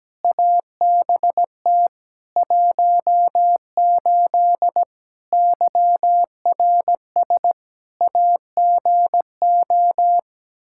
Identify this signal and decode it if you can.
Morse code, heard as a keyed tone, about 17 wpm, 700 Hz. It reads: ABT 18 YRS AGO
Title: K1USN SST QSO Practice 17wpm